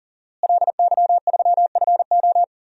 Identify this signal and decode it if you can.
Morse code, heard as a keyed tone, about 38 wpm, 690 Hz. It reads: LY3FO
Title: DXCC-100 Call Signs - 5 Characters - Repeated 2x - 40wpm